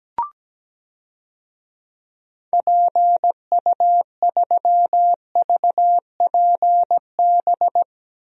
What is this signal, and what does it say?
Morse code, heard as a keyed tone, about 17 wpm, 700 Hz. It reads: PU3VPB
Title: DXCC-100 Call Signs - 6 Characters - 17wpm